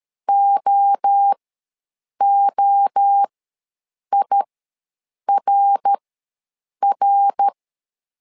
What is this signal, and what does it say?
Morse code, heard as a keyed tone, about 13 wpm, spaced as for 7 wpm, 800 Hz. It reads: OOIRR
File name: LEZIONI / 3 MWRO / 3 MWRO ws 12wpm 1914